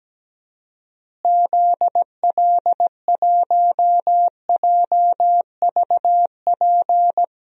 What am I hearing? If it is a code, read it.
Morse code, heard as a keyed tone, about 17 wpm, 700 Hz. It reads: ZL1JVP
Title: DXCC-100 Call Signs - 6 Characters - Repeated 3x - 17wpm